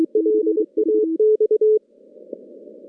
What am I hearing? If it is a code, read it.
Morse code, heard as a keyed tone, about 25 wpm, 430 Hz. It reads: LAX
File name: contests / CW-SS-k5tr- -2200